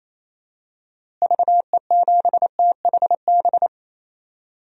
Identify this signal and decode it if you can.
Morse code, heard as a keyed tone, about 28 wpm, 700 Hz. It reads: VE7THB
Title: DXCC-100 Call Signs - 6 Characters - Repeated 2x - 28wpm